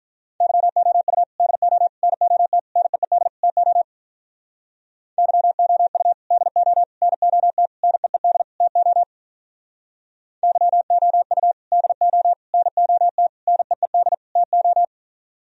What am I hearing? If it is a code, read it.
Morse code, heard as a keyed tone, about 44 wpm, 700 Hz. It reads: YOU DO NOT NEED TO YOU DO NOT NEED TO YOU DO NOT NEED TO